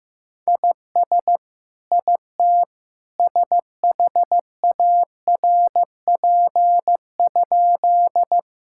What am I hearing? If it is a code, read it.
Morse code, heard as a keyed tone, about 15 wpm, 700 Hz. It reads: IS IT SHARP?
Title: Sentences from Top 1600 Words 15wpm